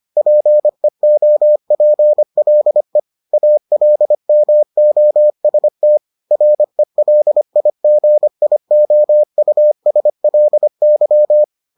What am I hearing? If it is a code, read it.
Morse code, heard as a keyed tone, about 25 wpm, 600 Hz. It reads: PEOPLE ALMOST RELIGIOUSLY